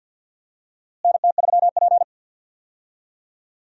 Morse code, heard as a keyed tone, about 53 wpm, 700 Hz. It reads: NT3P
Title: USA Call Signs - 4 Characters - Repeated 3x - 50wpm